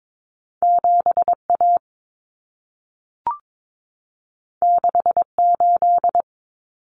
Morse code, heard as a keyed tone, about 22 wpm, 700 Hz. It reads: 7A 68
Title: Common Cut Numbers - 2-Digit 22wpm